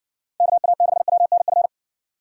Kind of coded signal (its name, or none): Morse code